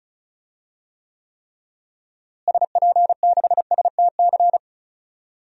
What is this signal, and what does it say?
Morse code, heard as a keyed tone, about 36 wpm, 700 Hz. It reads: SP6STC
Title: DXCC-100 Call Signs - 6 Characters - Repeated 3x - 35wpm